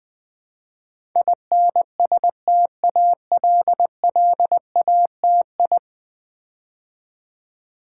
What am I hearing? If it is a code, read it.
Morse code, heard as a keyed tone, about 20 wpm, 700 Hz. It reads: INSTALLATI
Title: Build A Word - 5 Letters At A Time - 12-Letter Words - 20wpm